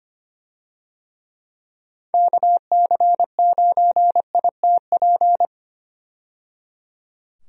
Morse code, heard as a keyed tone, about 25 wpm, 700 Hz. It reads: KC9ITP
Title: USA Call Signs - 6 Characters - 25wpm